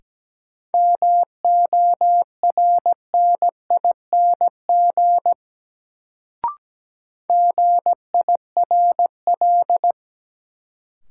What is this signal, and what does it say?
Morse code, heard as a keyed tone, about 17 wpm, 700 Hz. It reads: MORNING GIRL
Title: Sentences from Top 400 Words - New Word Drill - 17wpm